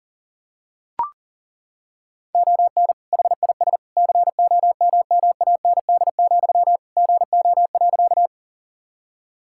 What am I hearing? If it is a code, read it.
Morse code, heard as a keyed tone, about 40 wpm, 700 Hz. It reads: ON HIS COMMAND, GO.
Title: Sentences from Top 1600 Words 40wpm